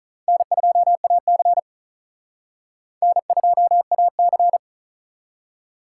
Morse code, heard as a keyed tone, about 42 wpm, 700 Hz. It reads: N2AC N2AC
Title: Speed Racing - USA 1x2 Call Signs 35wpm